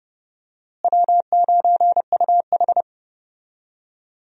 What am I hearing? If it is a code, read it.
Morse code, heard as a keyed tone, about 30 wpm, 700 Hz. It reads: W9UH
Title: USA Call Signs - 4 Characters - Repeated 2x - 30wpm